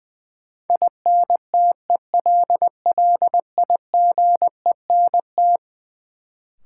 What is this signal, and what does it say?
Morse code, heard as a keyed tone, about 20 wpm, 700 Hz. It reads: INTELLIGENT
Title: Sentences from Top 1000 Words - Review Part 7 - New Word Drill - 20wpm